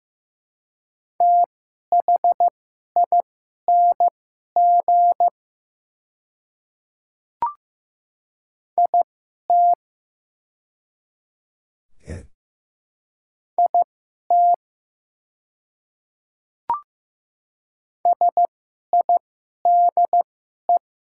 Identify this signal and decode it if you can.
Morse code, heard as a keyed tone, about 15 wpm, spaced as for 11 wpm, 700 Hz. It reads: THING IT IT SIDE
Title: 2x Character Spacing - Top 100 Words - 15wpm